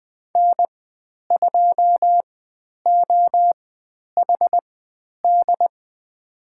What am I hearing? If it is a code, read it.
Morse code, heard as a keyed tone, about 20 wpm, spaced as for 10 wpm, 700 Hz. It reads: N2OHD
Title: 095 Lesson 16g - Letter c - Call Signs 20wpm